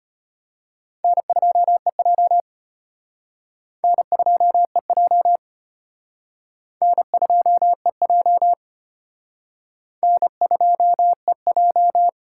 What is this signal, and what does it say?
Morse code, heard as a keyed tone, about 32 wpm, 700 Hz. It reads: N2EJ N2EJ N2EJ N2EJ